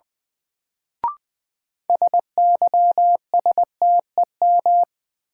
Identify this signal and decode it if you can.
Morse code, heard as a keyed tone, about 20 wpm, 700 Hz. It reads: SYSTEM
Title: Sentences from Top 1000 Words - Review Part 3 - New Word Drill - 20wpm